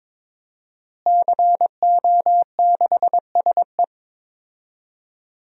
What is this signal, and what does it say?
Morse code, heard as a keyed tone, about 22 wpm, 700 Hz. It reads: CO6SE
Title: DXCC-100 Call Signs - 5 Characters - Repeated 2x - 22wpm